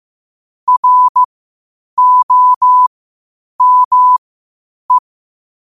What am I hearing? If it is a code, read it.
Morse code, heard as a keyed tone, about 15 wpm, spaced as for 8 wpm, 1000 Hz. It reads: ROME
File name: qotdfiles / 8wpm